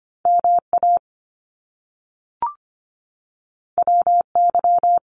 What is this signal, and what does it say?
Morse code, heard as a keyed tone, about 25 wpm, 700 Hz. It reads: MA WY
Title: US States 25wpm